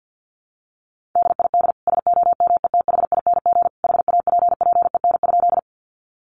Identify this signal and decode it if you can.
Morse code, heard as a keyed tone, about 50 wpm, 700 Hz. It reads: DID SOMETHING HAPPEN?
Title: Sentences from Top 600 Words 50wpm